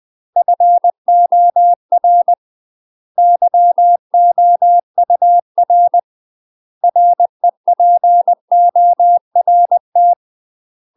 Morse code, heard as a keyed tone, about 20 wpm, 700 Hz. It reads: FOR YOUR REPORT
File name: UK MASTER QSOs TRAINING / 20WPM / UK MASTER QSOs 0032